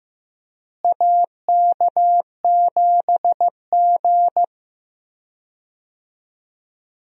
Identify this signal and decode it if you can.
Morse code, heard as a keyed tone, about 15 wpm, 700 Hz. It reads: AK7G
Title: USA Call Signs - 4 Characters - Repeated 2x - 15wpm